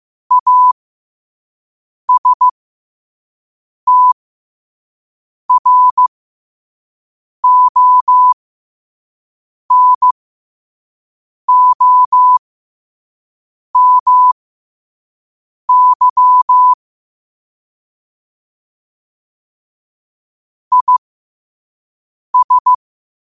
Morse code, heard as a keyed tone, about 15 wpm, spaced as for 5 wpm, 1000 Hz. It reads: ASTRONOMY IS
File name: qotdfiles / 5wpm